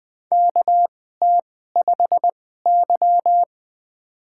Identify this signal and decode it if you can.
Morse code, heard as a keyed tone, about 20 wpm, spaced as for 15 wpm, 700 Hz. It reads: KT5Y
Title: USA Call Signs - 4 Characters - 2x Character Spacing - 20wpm